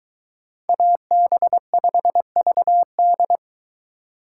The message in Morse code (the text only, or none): AB5VD